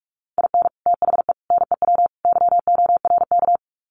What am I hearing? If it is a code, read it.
Morse code, heard as a keyed tone, about 44 wpm, 700 Hz. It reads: IN THE NEW YORK
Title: Sets of 4 Words 45wpm